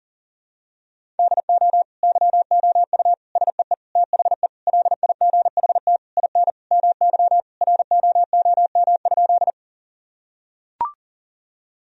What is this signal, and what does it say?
Morse code, heard as a keyed tone, about 37 wpm, 700 Hz. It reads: DO YOU SEE THE LIGHT IN MY ROOM?
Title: Sentences from Top 1000 Words - Review Part 1 - 40wpm